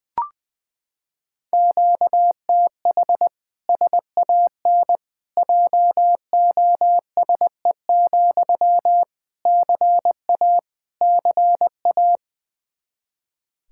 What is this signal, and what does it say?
Morse code, heard as a keyed tone, about 21 wpm, 700 Hz. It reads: QTH SAN JOSE, CA CA
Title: Practice QSO - Exchange 1 - Beginning - 20wpm